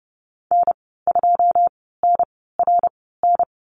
Morse code, heard as a keyed tone, about 30 wpm, spaced as for 17 wpm, 700 Hz. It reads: N2NRN
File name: Lesson 14g - Number 2 - Call Signs 30wpm